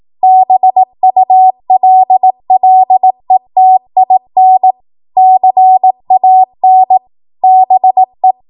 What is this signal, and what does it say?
Morse code, heard as a keyed tone, about 18 wpm, 750 Hz. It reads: BULLETIN CAN BE
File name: Morse / 18WPM